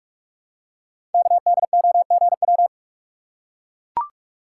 Morse code, heard as a keyed tone, about 44 wpm, 700 Hz. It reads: KDOGW
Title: Code Groups - 5 Letters - 45wpm